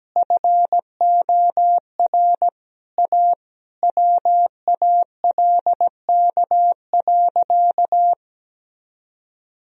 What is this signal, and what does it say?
Morse code, heard as a keyed tone, about 17 wpm, 700 Hz. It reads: FOR A WALK.
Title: Sentences from Top 1000 Words - Review Part 1 - 17wpm